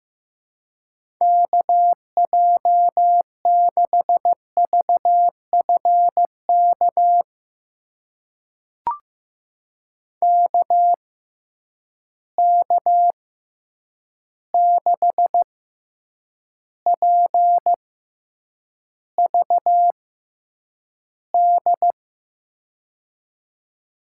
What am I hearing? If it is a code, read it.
Morse code, heard as a keyed tone, about 15 wpm, spaced as for 8 wpm, 700 Hz. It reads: KJ6VFK KK6PVD
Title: USA Call Signs - 6 Characters - 6x Character Spacing - 15wpm